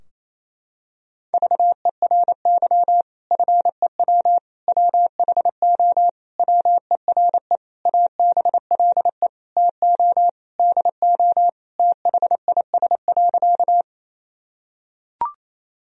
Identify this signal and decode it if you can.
Morse code, heard as a keyed tone, about 27 wpm, 700 Hz. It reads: VERY FEW WHO WERE ABLE TO DO THIS.
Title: Sentences from Top 1000 Words - Review Part 2 - 28wpm